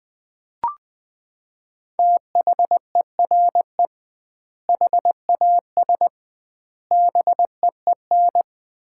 Morse code, heard as a keyed tone, about 21 wpm, 700 Hz. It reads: THERE HAS BEEN
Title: 2x Word-Spacing - Sets of 4 Words - 20wpm